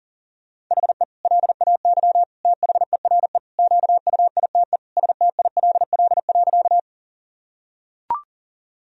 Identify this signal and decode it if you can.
Morse code, heard as a keyed tone, about 40 wpm, 700 Hz. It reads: HE LAY THERE QUITE STILL.
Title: Sentences from Top 1000 Words - Review Part 2 - 40wpm